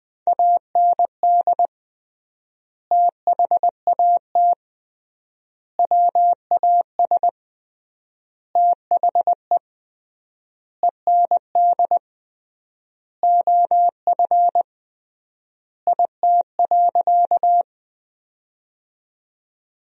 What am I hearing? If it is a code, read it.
Morse code, heard as a keyed tone, about 20 wpm, 700 Hz. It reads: AND THAT WAS THE END OF IT.
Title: 3x Word-Spacing - Sentences from Top 200 Words - 20wpm